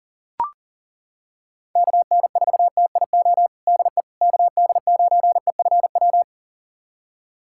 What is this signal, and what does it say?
Morse code, heard as a keyed tone, about 40 wpm, 700 Hz. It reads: KN4TIO DE KD9EFW
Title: USA Callsign DE Callsign - 6 Characters - 40wpm